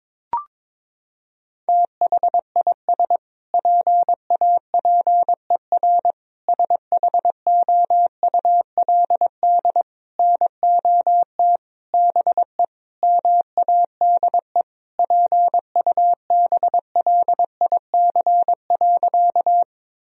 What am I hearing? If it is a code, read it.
Morse code, heard as a keyed tone, about 22 wpm, 700 Hz. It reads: THIS PAPER SHOULD NOT BE MADE PUBLIC.
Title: Sentences from Top 1000 Words - Review Part 3 - 22wpm